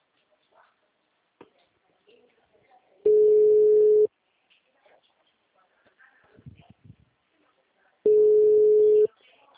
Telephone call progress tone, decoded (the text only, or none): ringback tone